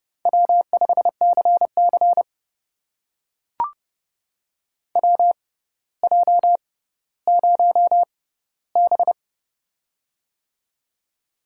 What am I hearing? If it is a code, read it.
Morse code, heard as a keyed tone, about 30 wpm, spaced as for 15 wpm, 700 Hz. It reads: W5CC WJ0B
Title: USA Call Signs - 6x Character Spacing - 30wpm